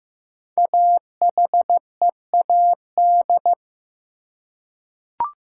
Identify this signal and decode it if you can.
Morse code, heard as a keyed tone, about 15 wpm, 700 Hz. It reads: AHEAD
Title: Five Letter Words - 15wpm